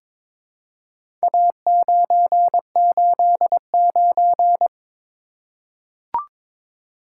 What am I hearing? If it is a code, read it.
Morse code, heard as a keyed tone, about 22 wpm, 700 Hz. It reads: A989
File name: practice / 22wpm / Four Letter-Number Combinations - 22wpm